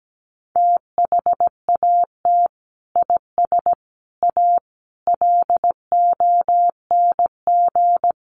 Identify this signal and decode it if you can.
Morse code, heard as a keyed tone, about 17 wpm, 700 Hz. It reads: THAT IS A LONG
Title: Sentences from Top 100 Words 17wpm